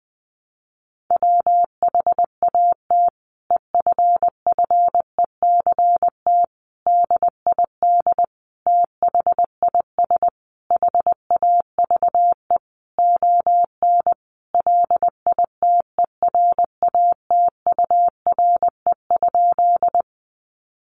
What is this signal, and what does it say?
Morse code, heard as a keyed tone, about 20 wpm, 700 Hz. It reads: WHAT EFFECT DID THIS HAVE ON LITERATURE?
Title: Sentences from Top 1000 Words 20wpm